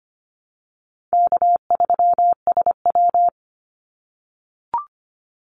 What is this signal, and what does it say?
Morse code, heard as a keyed tone, about 25 wpm, 700 Hz. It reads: K3SW
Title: Call Signs - 25wpm